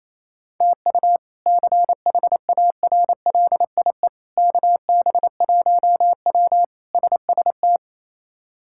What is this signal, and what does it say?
Morse code, heard as a keyed tone, about 28 wpm, 700 Hz. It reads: TU CHARLIE KB1W SST
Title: K1USN SST QSO Practice - 28wpm